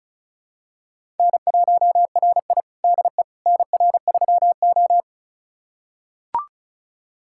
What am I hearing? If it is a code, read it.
Morse code, heard as a keyed tone, about 36 wpm, 700 Hz. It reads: N1RI DE NR3O
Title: USA Callsign DE Callsign - 4 Characters - 35wpm